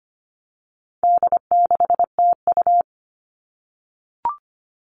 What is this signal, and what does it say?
Morse code, heard as a keyed tone, about 25 wpm, 700 Hz. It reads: D6TU
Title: All Cut Numbers - 4-Digit 25wpm